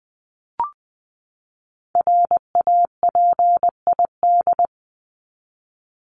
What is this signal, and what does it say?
Morse code, heard as a keyed tone, about 20 wpm, 700 Hz. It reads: RAPID
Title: LY Suffix - Top 100 Words with Base Words 20wpm